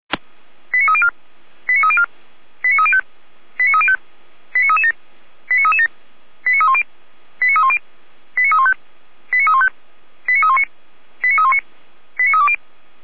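Five-tone selective call (ZVEI), heard as quick runs of five tones, tones of about 70 ms. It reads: 89304 89304 89305 89305 89337 89337 89310 89310 89315 89315 89319 89319 89320
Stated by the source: SDR-received ZVEI five-tone call, fire brigade Orlamünde (feuerwehr-orlamuende).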